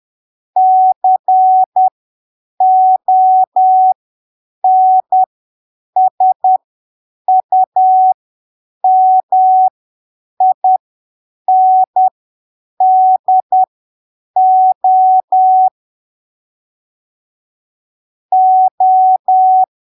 Morse code, heard as a keyed tone, about 10 wpm, spaced as for 7 wpm, 750 Hz. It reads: CONSUMINDO O
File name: arquivos / texto / texto 19 10ppm